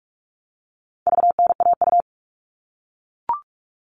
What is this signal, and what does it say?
Morse code, heard as a keyed tone, about 44 wpm, 700 Hz. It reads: VNAU
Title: All Cut Numbers - 4-Digit 45wpm